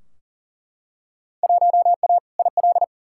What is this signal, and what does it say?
Morse code, heard as a keyed tone, about 40 wpm, 700 Hz. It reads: 1A IL
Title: Field Day Exchanges Encore - 40wpm